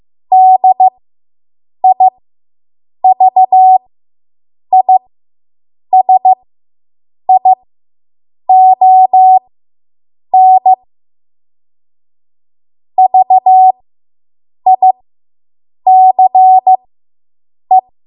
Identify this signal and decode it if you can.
Morse code, heard as a keyed tone, about 15 wpm, spaced as for 7 wpm, 750 Hz. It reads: DIVISION VICE